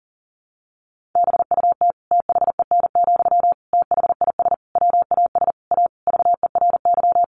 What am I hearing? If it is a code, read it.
Morse code, heard as a keyed tone, about 40 wpm, 700 Hz. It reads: BUT THEN, THIS WAS A VERY
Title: Sentences from Top 1400 Words 40wpm